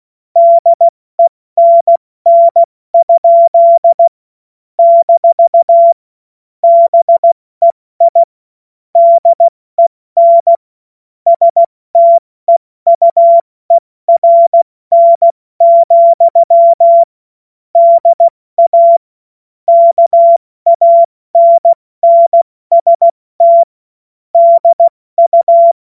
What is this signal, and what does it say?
Morse code, heard as a keyed tone, about 16 wpm, 680 Hz. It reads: DENN? - BEI DEN STEUERN, DA KANNST DU
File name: Witz-28-Oktober-2025-70-F80